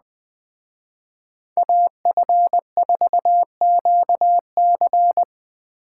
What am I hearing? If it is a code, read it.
Morse code, heard as a keyed tone, about 20 wpm, 700 Hz. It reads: AF4QC